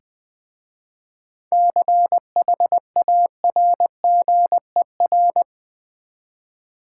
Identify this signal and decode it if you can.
Morse code, heard as a keyed tone, about 20 wpm, 700 Hz. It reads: CHARGER
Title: Easy Words with Hints - 20wpm